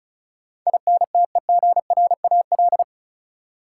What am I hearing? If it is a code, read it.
Morse code, heard as a keyed tone, about 36 wpm, 700 Hz. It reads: INTEGRAL
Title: Sets of 5 Words - New Word Drill - 35wpm